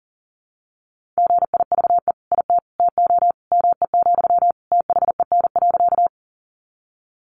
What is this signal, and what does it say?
Morse code, heard as a keyed tone, about 40 wpm, 700 Hz. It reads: GIVE IT TO ME, THEN.
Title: Sentences from Top 200 Words 40wpm